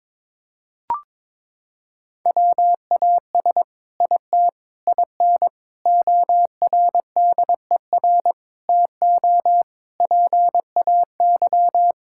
Morse code, heard as a keyed tone, about 22 wpm, 700 Hz. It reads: WAS IT IN ORDER TO PAY
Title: Sentences from Top 1000 Words - Review Part 6 - 22wpm